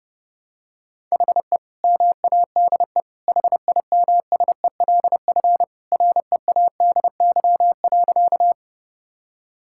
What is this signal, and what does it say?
Morse code, heard as a keyed tone, about 30 wpm, 700 Hz. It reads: HE MADE HIMSELF READY.
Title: Sentences from Top 1600 Words 30wpm